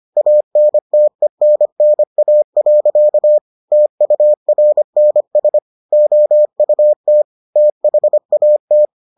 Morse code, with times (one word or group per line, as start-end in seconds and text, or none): 0.2-3.4 ANTENNA.
3.7-5.6 TURNS
5.9-7.2 OUT
7.6-8.9 THAT